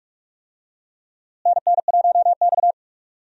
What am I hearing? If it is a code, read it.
Morse code, heard as a keyed tone, about 45 wpm, 700 Hz. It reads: NN1X
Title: USA Call Signs - 4 Characters - Repeated 2x - 45wpm